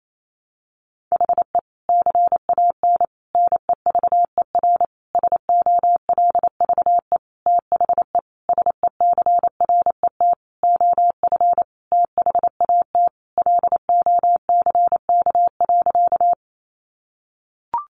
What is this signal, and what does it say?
Morse code, heard as a keyed tone, about 28 wpm, 700 Hz. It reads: HE CAN NEVER SOLVE THE SECRET OF THAT LOCK.
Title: Sentences from Top 1300 Words 28wpm